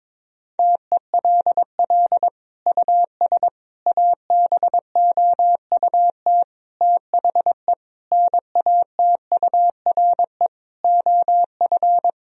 Morse code, heard as a keyed tone, about 22 wpm, 700 Hz. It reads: TELL US ABOUT THE NATURE OF
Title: Sentences from Top 1000 Words - Review Part 4 - 22wpm